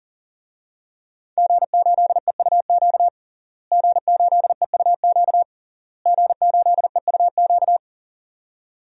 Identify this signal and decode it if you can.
Morse code, heard as a keyed tone, about 40 wpm, 700 Hz. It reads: G8EUQ G8EUQ G8EUQ
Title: DXCC-100 Call Signs - 5 Characters - Repeated 3x - 40wpm